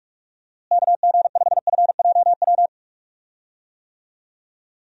Morse code, heard as a keyed tone, about 43 wpm, 700 Hz. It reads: KG5FJW
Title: USA Call Signs - 6 Characters - Repeated 2x - 45wpm